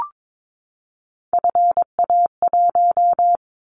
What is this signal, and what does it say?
Morse code, heard as a keyed tone, about 22 wpm, 700 Hz. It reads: FA1
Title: Callsign Prefixes for Top 25 DXCC - 22wpm